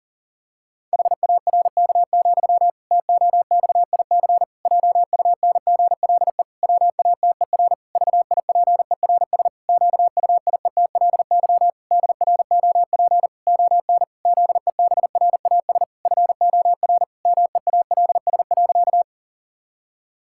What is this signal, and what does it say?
Morse code, heard as a keyed tone, about 37 wpm, 700 Hz. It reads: HARK, TOXIC JUNGLE WATER VIPERS QUIETLY DROP ON ZEBRAS FOR MEALS.
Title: Pangram Challenge v2 - 40wpm